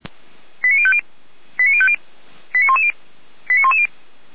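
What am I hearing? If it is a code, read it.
Five-tone selective call (ZVEI), heard as quick runs of five tones, tones of about 70 ms. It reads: 89050 89050 89110 89110
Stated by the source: ZVEI five-tone call received on SDR.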